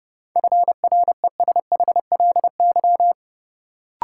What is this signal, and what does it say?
Morse code, heard as a keyed tone, about 32 wpm, 700 Hz. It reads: FRESHLY
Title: Sets of 5 Words Encore - New Word Drill - 30wpm